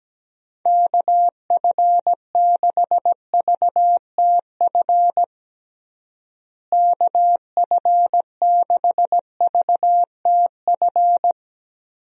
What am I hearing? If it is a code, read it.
Morse code, heard as a keyed tone, about 17 wpm, 700 Hz. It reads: KF6VTF KF6VTF
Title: USA Call Signs - 6 Characters - Repeated 3x - 17wpm